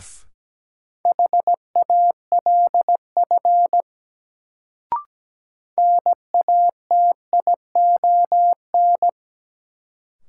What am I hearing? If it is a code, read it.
Morse code, heard as a keyed tone, about 17 wpm, 700 Hz. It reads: HALF NATION
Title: Sets of 3 Words - New Word Drill - 17wpm